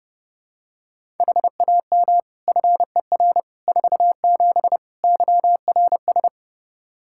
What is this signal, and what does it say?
Morse code, heard as a keyed tone, about 29 wpm, 700 Hz. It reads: HAM FER 47 YRS
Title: K1USN SST QSO Practice 30wpm